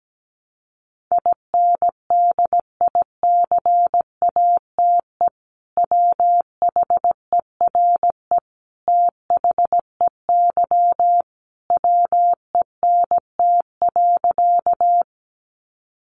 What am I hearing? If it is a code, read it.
Morse code, heard as a keyed tone, about 17 wpm, 700 Hz. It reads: INDICATE WHERE THEY WENT.
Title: Sentences from Top 1300 Words 17wpm